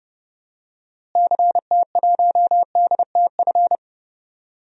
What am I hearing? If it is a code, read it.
Morse code, heard as a keyed tone, about 30 wpm, 700 Hz. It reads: CT1DTF
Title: DXCC-100 Call Signs - 6 Characters - Repeated 2x - 30wpm